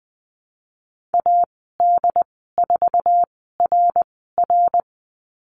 Morse code, heard as a keyed tone, about 20 wpm, spaced as for 15 wpm, 700 Hz. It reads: AD4RR
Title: 076 Lesson 13g - Letter L - Call Signs 20wpm